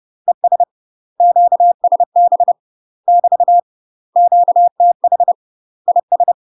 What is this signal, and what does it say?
Morse code, heard as a keyed tone, about 30 wpm, 700 Hz. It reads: ES QSB <BT> QTH IS
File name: UK MASTER QSOs TRAINING / 30WPM / UK MASTER QSOs 0013